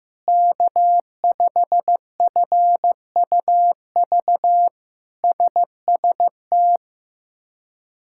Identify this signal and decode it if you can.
Morse code, heard as a keyed tone, about 15 wpm, 700 Hz. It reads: K5FUV SST
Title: K1USN SST QSO Practice - 15wpm